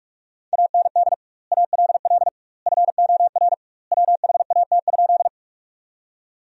Morse code, heard as a keyed tone, about 46 wpm, 700 Hz. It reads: AND ALL FOR WHAT?
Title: 2x Word-Spacing - Sentences from Top 100 Words - 45wpm